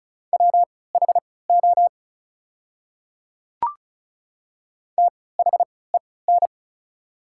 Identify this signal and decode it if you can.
Morse code, heard as a keyed tone, about 36 wpm, spaced as for 20 wpm, 700 Hz. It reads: WHO THEN
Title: 3x Character Spacing - Top 100 Words - 35wpm